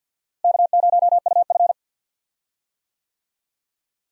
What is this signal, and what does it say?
Morse code, heard as a keyed tone, about 53 wpm, 700 Hz. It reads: K0UF
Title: USA Call Signs - 4 Characters - Repeated 3x - 50wpm